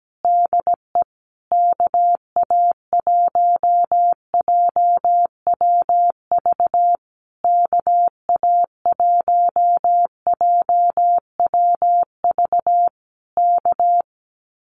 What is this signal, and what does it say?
Morse code, heard as a keyed tone, about 17 wpm, 700 Hz. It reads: DE KA1JWV KA1JWV K
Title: Common QSO Phrases Encore 17wpm